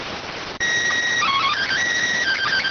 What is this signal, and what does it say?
SSTV (slow-scan television), PD 290.